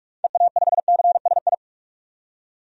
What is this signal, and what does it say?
Morse code, heard as a keyed tone, about 42 wpm, 700 Hz. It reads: EA5CSI
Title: DXCC-100 Call Signs - 6 Characters - 45wpm